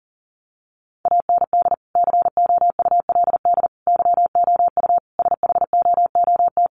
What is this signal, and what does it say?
Morse code, heard as a keyed tone, about 40 wpm, 700 Hz. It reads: AND COULD YOU SHOOT
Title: Sentences from Top 1200 Words 40wpm